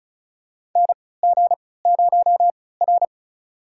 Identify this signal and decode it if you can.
Morse code, heard as a keyed tone, about 36 wpm, spaced as for 20 wpm, 700 Hz. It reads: NG0R
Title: USA Call Signs - 4 Characters - 3x Character Spacing - 35wpm